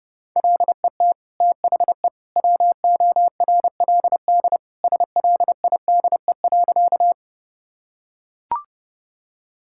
Morse code, heard as a keyed tone, about 32 wpm, 700 Hz. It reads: LET THE WORLD SLIDE.
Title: Sentences from Top 1600 Words 30wpm